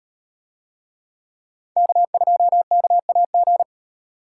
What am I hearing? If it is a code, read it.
Morse code, heard as a keyed tone, about 38 wpm, 700 Hz. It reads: K2KAG